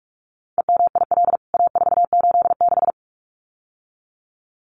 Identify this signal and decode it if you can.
Morse code, heard as a keyed tone, about 44 wpm, 700 Hz. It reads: EMIL A486
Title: CWT Exchange 45wpm